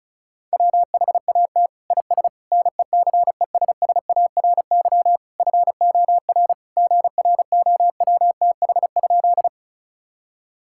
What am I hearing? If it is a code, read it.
Morse code, heard as a keyed tone, about 34 wpm, 700 Hz. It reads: WHAT IS NECESSARY FOR GROWTH?
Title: Sentences from Top 1000 Words - Review Part 5 - 35wpm